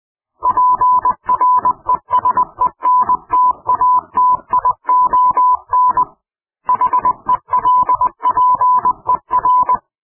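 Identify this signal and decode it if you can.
Morse code, heard as a keyed tone, about 20 wpm, 1000 Hz. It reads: PRESENTATION HELPER
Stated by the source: Morse code clip offered as an mp3 ring tone, taken from PowerPoint sound clips.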